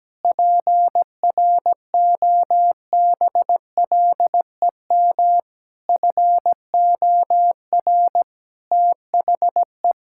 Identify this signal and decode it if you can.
Morse code, heard as a keyed tone, about 17 wpm, 700 Hz. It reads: PROBLEM FOR THE
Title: Sentences from Top 1000 Words - Review Part 5 - 17wpm